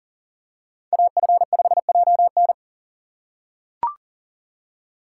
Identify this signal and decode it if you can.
Morse code, heard as a keyed tone, about 40 wpm, 700 Hz. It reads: AF5JN
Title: USA Call Signs - 5 Characters - 3x Character Spacing - 40wpm